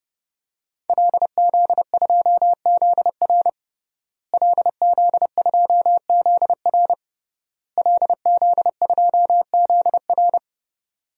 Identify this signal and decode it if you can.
Morse code, heard as a keyed tone, about 29 wpm, 700 Hz. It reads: LZ2ZR LZ2ZR LZ2ZR
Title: DXCC-100 Call Signs - 5 Characters - Repeated 3x - 30wpm